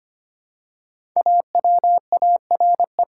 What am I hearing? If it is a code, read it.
Morse code, heard as a keyed tone, about 25 wpm, 700 Hz. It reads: AWARE
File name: 5 Letter Words - First 3 Letters Given - 25wpm